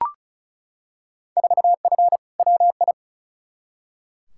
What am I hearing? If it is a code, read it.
Morse code, heard as a keyed tone, about 36 wpm, 700 Hz. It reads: 4F WI
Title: Field Day Exchanges Encore - 35wpm